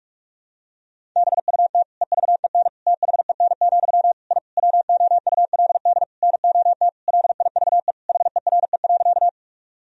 Morse code, heard as a keyed tone, about 46 wpm, 700 Hz. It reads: BUT EVEN THEN, I WOULD NOT LIVE HERE.
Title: Sentences from Top 1000 Words - Review Part 1 - 45wpm